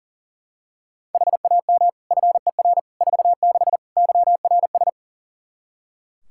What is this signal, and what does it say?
Morse code, heard as a keyed tone, about 40 wpm, 700 Hz. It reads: HAM FER 46 YRS
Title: Practice QSO - Exchange 3 - More Info - 40wpm